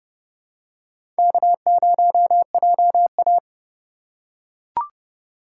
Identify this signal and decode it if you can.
Morse code, heard as a keyed tone, about 32 wpm, 700 Hz. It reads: K0JA
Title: USA Call Signs - 4 Characters - 3x Character Spacing - 30wpm